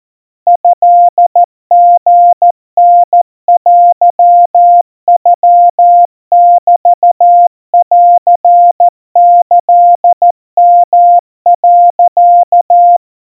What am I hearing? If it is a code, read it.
Morse code (Japanese Wabun code), heard as a keyed tone, about 14 wpm, 700 Hz. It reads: トリタテノメンキヨ、
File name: kamyu 50 cpm 700 Hz Crispy